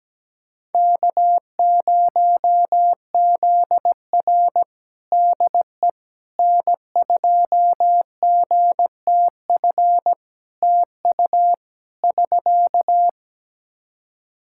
Morse code, heard as a keyed tone, about 17 wpm, 700 Hz. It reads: K0ZR DE N2GTF TU <SK>
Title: Practice QSO - Exchange 4 - Ending - 17wpm